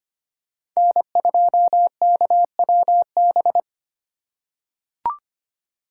Morse code, heard as a keyed tone, about 25 wpm, 700 Hz. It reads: N2KWB
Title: USA Call Signs - 5 Characters - 2x Character Spacing - 25wpm